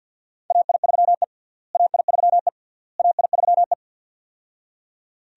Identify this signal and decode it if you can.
Morse code, heard as a keyed tone, about 52 wpm, 700 Hz. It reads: AI3E AI3E AI3E
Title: USA Call Signs - 4 Characters - Repeated 3x - 50wpm